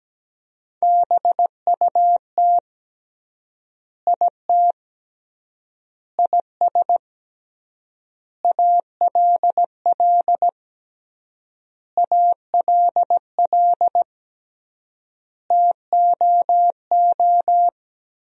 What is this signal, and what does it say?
Morse code, heard as a keyed tone, about 17 wpm, 700 Hz. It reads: BUT IT IS ALL ALL TOO
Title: 3x Word-Spacing - Sentences from Top 300 Words - 17wpm